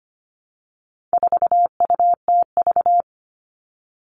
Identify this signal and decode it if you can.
Morse code, heard as a keyed tone, about 25 wpm, 700 Hz. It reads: 4UTV
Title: All Cut Numbers - 4-Digit 25wpm